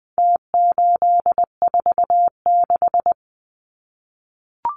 Morse code, heard as a keyed tone, about 20 wpm, 700 Hz. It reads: T846
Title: Numbers - Contest Style - 4-Digit 20wpm